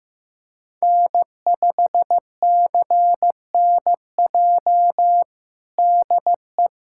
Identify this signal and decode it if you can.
Morse code, heard as a keyed tone, about 15 wpm, 700 Hz. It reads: N5CNJ DE
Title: USA Callsign DE Callsign - 5 Characters - 15wpm